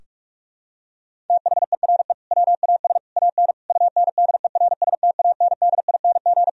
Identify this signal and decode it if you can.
Morse code, heard as a keyed tone, about 46 wpm, 700 Hz. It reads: THERE WAS AN UNDERSTANDING
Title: Sentences from Top 1000 Words - Review Part 3 - 45wpm